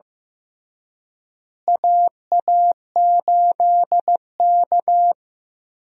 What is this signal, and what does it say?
Morse code, heard as a keyed tone, about 15 wpm, 700 Hz. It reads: AA8K
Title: USA Call Signs - 4 Characters - Repeated 3x - 15wpm